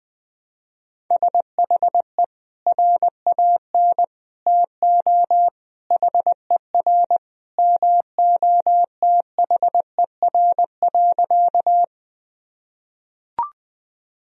Morse code, heard as a keyed tone, about 20 wpm, 700 Hz. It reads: SHE RAN TO HER MOTHER.
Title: Sentences from Top 1000 Words - Review Part 3 - 20wpm